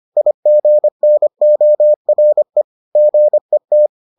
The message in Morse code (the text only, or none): IGNORE GET